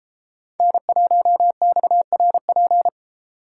Morse code, heard as a keyed tone, about 30 wpm, 700 Hz. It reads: N1XRP